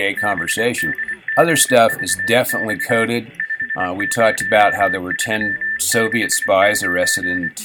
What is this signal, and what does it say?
Morse code, heard as a keyed tone, about 24 wpm, 1800 Hz, with speech. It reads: BDGVX 50369